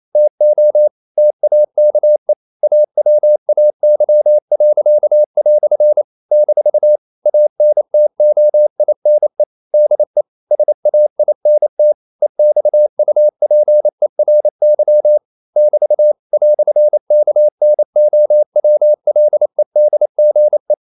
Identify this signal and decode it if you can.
Morse code, heard as a keyed tone, about 28 wpm, 600 Hz. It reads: TO TAKE AWAY." - ANTOINE DE SAINT EXUPERY <BT> "KNOWLEDGE
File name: CW / Practice / EnglishQuotations0228